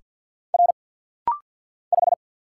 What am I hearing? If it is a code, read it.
Morse code, heard as a keyed tone, about 50 wpm, 690 Hz. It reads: R 5
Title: Single Letter-Number Alt - Rapid-Fire 50wpm